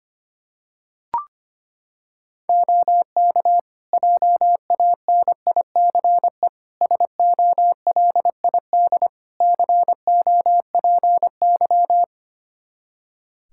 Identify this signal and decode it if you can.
Morse code, heard as a keyed tone, about 25 wpm, 700 Hz. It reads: OK JANICE SOLID COPY
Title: K1USN SST QSO Practice 25wpm